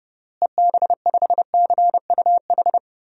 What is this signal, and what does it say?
Morse code, heard as a keyed tone, about 30 wpm, 700 Hz. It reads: EB5CUH
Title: DXCC-100 Call Signs - 6 Characters - Repeated 3x - 30wpm